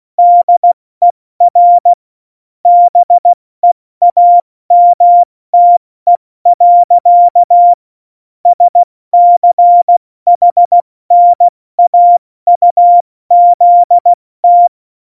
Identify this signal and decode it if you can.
Morse code, heard as a keyed tone, about 16 wpm, 700 Hz. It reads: DER BEAMTE. SCHNAUZT
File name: Morsekurs / Witze / Witz 25. Februar 2020, 70 F80